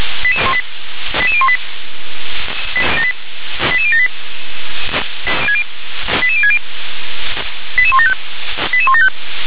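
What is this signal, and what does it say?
Five-tone selective call (ZVEI), heard as quick runs of five tones, tones of about 70 ms. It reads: 88018 88078 88060 88175 88175
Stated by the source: ZVEI five-tone selective call received on an SDR.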